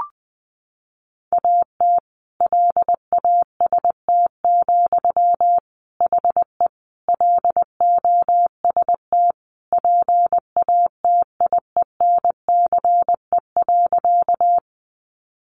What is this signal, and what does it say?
Morse code, heard as a keyed tone, about 20 wpm, 700 Hz. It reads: AT LAST, HE LOST PATIENCE.
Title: Sentences from Top 1400 Words 20wpm